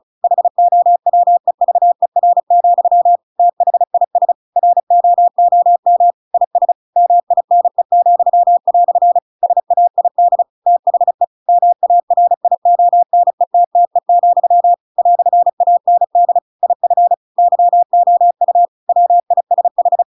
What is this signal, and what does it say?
Morse code, heard as a keyed tone, about 34 wpm, 700 Hz. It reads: HOWEVER, THIS ROOM IS MINE," SAID THE MARIONETTE, "AND IF YOU WISH